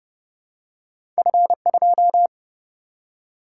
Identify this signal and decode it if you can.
Morse code, heard as a keyed tone, about 30 wpm, 700 Hz. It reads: F2
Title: Single-Letter Single-Number 30wpm